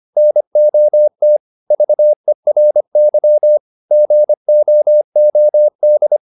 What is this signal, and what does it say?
Morse code, heard as a keyed tone, about 25 wpm, 600 Hz. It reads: NOT VERY GOOD